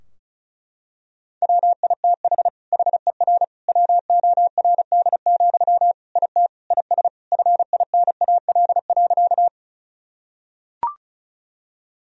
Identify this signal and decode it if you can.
Morse code, heard as a keyed tone, about 36 wpm, 700 Hz. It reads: WITH HER WORD, IT IS FINAL.
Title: Sentences from Top 1000 Words - Review Part 2 - 35wpm